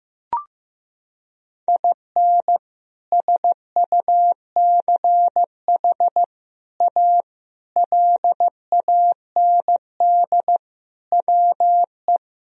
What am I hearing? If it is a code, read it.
Morse code, heard as a keyed tone, about 15 wpm, 700 Hz. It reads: IN SUCH A LAND WE
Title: Sentences from Top 1000 Words - Review Part 1 - 15wpm